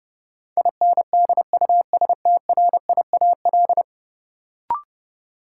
Morse code, heard as a keyed tone, about 30 wpm, 700 Hz. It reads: INDUSTRIAL
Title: Sets of 5 Words - New Word Drill - 30wpm